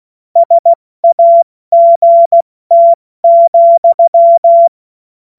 Morse code, heard as a keyed tone, about 16 wpm, spaced as for 14 wpm, 680 Hz. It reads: SAGT,
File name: Morsekurs / Witze / Witz 25. Mai 2021, 70 F80